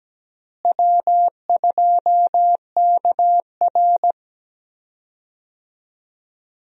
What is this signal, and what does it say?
Morse code, heard as a keyed tone, about 17 wpm, 700 Hz. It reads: W2KR
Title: USA Call Signs - 4 Characters - Repeated 2x - 17wpm